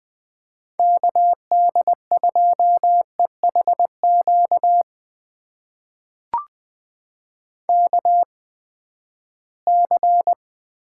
Morse code, heard as a keyed tone, about 20 wpm, spaced as for 14 wpm, 700 Hz. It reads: KD2EHQ KC
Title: USA Call Signs - 6 Characters - 8x Character Spacing - 20wpm